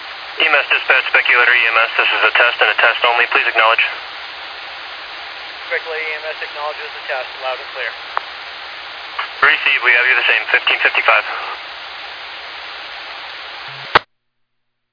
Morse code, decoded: none

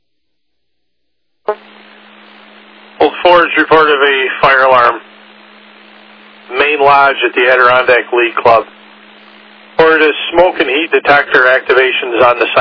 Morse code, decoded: none

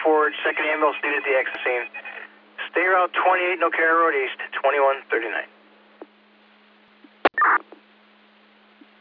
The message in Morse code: none

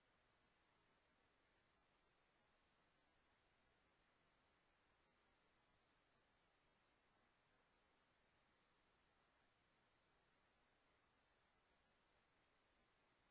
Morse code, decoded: none

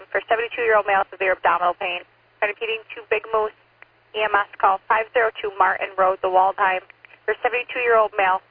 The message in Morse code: none

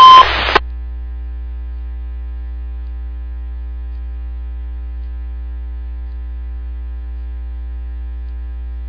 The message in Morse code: none

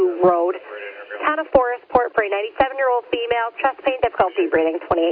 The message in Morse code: none

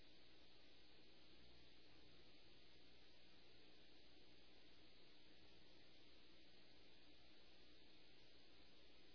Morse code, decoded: none

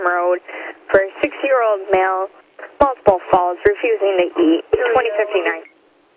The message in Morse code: none

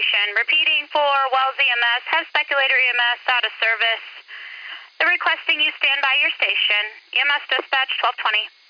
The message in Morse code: none

